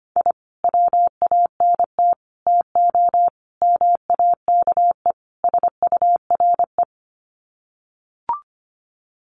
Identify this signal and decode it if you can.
Morse code, heard as a keyed tone, about 25 wpm, 700 Hz. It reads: I WANT TO MAKE SURE
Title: Sets of 5 Words Encore2 25wpm